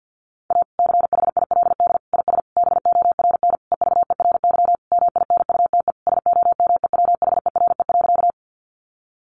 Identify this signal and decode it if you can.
Morse code, heard as a keyed tone, about 50 wpm, 700 Hz. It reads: A CHILD IS BORN EVERY MINUTE SOMEWHERE.
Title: Sentences from Top 1400 Words 50wpm